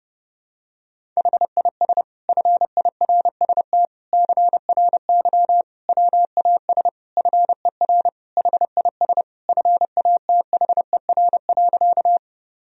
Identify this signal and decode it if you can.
Morse code, heard as a keyed tone, about 31 wpm, 700 Hz. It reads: HIS FIRST CRY WAS FER HIS FATHER.
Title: Sentences from Top 500 Words and Abbreviations 30wpm